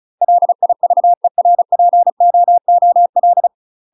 Morse code, heard as a keyed tone, about 33 wpm, 700 Hz. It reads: LIVERPOOL
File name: UK MASTER QSOs TRAINING / 35WPM / UK MASTER QSOs 0017